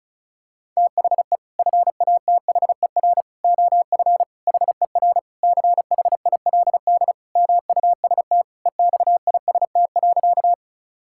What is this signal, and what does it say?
Morse code, heard as a keyed tone, about 34 wpm, 700 Hz. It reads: THE FATHER OF HER CHILD MUST EXIST.
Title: Sentences from Top 1000 Words - Review Part 5 - 35wpm